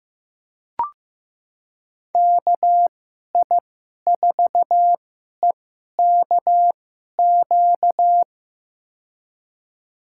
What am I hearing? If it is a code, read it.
Morse code, heard as a keyed tone, about 15 wpm, spaced as for 11 wpm, 700 Hz. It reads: KI4EKQ
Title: USA Call Signs - 6 Characters - 2x Character Spacing - 15wpm